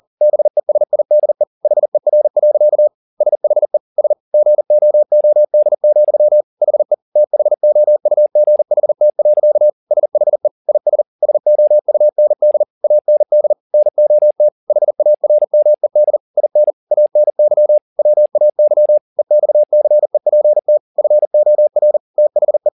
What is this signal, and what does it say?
Morse code, heard as a keyed tone, about 40 wpm, 600 Hz. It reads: BESIDE HER. SHE S GOOD, HE THOUGHT. SHE IS SOUND AND NOT HARMED IN ANY WAY EXCEPT FOR THE
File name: the old man and the sea 2 40wpm